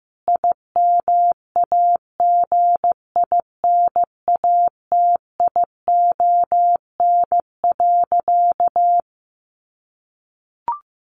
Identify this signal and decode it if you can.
Morse code, heard as a keyed tone, about 14 wpm, 700 Hz. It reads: IMAGINATION.
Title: Sentences from Top 1300 Words 15wpm